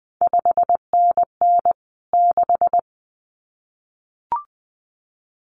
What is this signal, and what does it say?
Morse code, heard as a keyed tone, about 20 wpm, 700 Hz. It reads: 5NN 6
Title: CQ WW DX Contest Exchange Encore - 20wpm